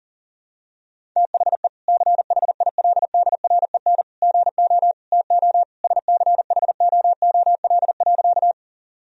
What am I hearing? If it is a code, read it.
Morse code, heard as a keyed tone, about 40 wpm, 700 Hz. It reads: THE CHILDREN GO TO SCHOOL.
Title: Sentences from Top 1000 Words - Review Part 1 - 40wpm